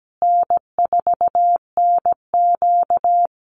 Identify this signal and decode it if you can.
Morse code, heard as a keyed tone, about 17 wpm, 700 Hz. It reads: N4NQ
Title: Call Signs Repeated 3x - 17wpm